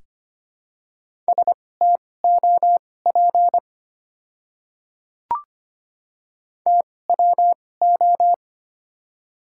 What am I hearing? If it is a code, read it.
Morse code, heard as a keyed tone, about 25 wpm, spaced as for 18 wpm, 700 Hz. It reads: STOP TWO
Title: 2x Character Spacing - Top 300 Words - 25wpm